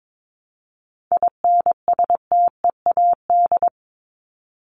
Morse code, heard as a keyed tone, about 22 wpm, 700 Hz. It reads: INSTEAD
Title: Top 501-750 Words 22wpm